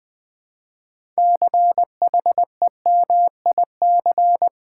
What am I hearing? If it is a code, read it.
Morse code, heard as a keyed tone, about 20 wpm, 700 Hz. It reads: CHEMIC
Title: Build A Word - 3 Letters At A Time - 8-Letter Words - 20wpm